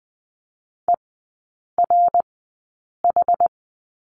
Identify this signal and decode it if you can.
Morse code, heard as a keyed tone, about 19 wpm, spaced as for 8 wpm, 690 Hz. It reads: ERH
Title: 060 Lesson 11e - Letter h - Three Characters 20wpm